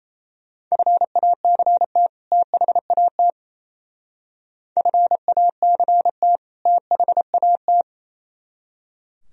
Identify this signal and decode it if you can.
Morse code, heard as a keyed tone, about 32 wpm, 700 Hz. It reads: FACT THAT FACT THAT
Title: Speed-Racing - Sets of 2 Words 28wpm